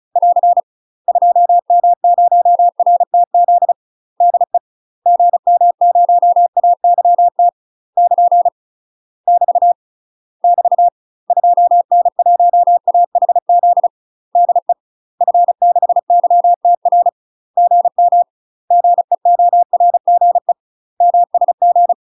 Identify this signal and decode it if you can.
Morse code, heard as a keyed tone, about 36 wpm, 700 Hz. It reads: <AR> 2M0RTZ DE GM0AYT <KN> <BT> <BT> 2N1AHZ DE F6YTR GM GEORGE MSG
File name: UK MASTER QSOs TRAINING / 35WPM / UK MASTER QSOs 0039